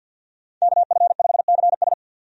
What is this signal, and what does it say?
Morse code, heard as a keyed tone, about 52 wpm, 700 Hz. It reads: KF5CS
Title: USA Call Signs - 5 Characters - 50wpm